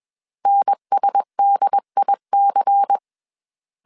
Morse code, heard as a keyed tone, about 21 wpm, 800 Hz. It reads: NSDIC
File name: CW / DET / 1 6 1500C S C 20wpm 2116